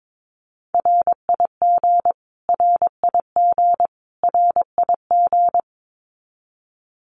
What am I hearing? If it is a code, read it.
Morse code, heard as a keyed tone, about 22 wpm, 700 Hz. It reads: RIG RIG RIG
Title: Abbreviations Repeated 3x - 22wpm